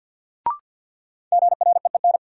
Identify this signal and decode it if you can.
Morse code, heard as a keyed tone, about 49 wpm, 700 Hz. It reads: GREEN
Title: Sentences from Top 500 Words - New Word Drill - 50wpm